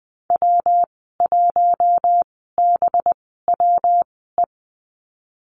Morse code, heard as a keyed tone, about 19 wpm, spaced as for 14 wpm, 700 Hz. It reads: W1BWE
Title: 186 Lesson 30g - Letter b - Call Signs 20wpm